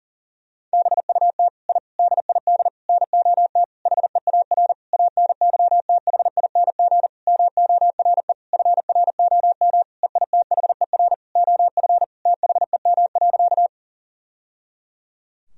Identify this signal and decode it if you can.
Morse code, heard as a keyed tone, about 37 wpm, 700 Hz. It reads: BUT I DID NOT HEAR ANYTHING MORE FROM EITHER OF THEM.
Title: Sentences from Top 1000 Words - Review Part 4 - 40wpm